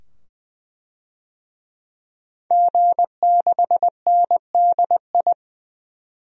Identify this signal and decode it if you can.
Morse code, heard as a keyed tone, about 21 wpm, 700 Hz. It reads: G6NDI
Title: DXCC-100 Call Signs - 5 Characters - 20wpm